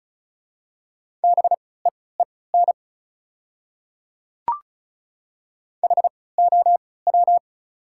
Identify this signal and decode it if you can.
Morse code, heard as a keyed tone, about 36 wpm, spaced as for 20 wpm, 690 Hz. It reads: BEEN HOW
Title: 3x Character Spacing - Top 100 Words - 35wpm